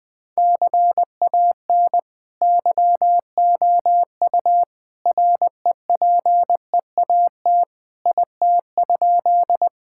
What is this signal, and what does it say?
Morse code, heard as a keyed tone, about 20 wpm, 700 Hz. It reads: CAN YOU REPEAT IT?
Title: Sentences from Top 1600 Words 20wpm